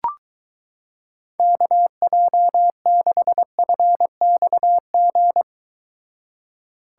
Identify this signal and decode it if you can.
Morse code, heard as a keyed tone, about 23 wpm, 700 Hz. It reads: KJ6FXG